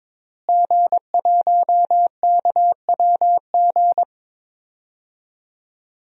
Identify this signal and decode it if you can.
Morse code, heard as a keyed tone, about 21 wpm, 700 Hz. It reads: G1KWG
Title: DXCC-100 Call Signs - 5 Characters - Repeated 2x - 22wpm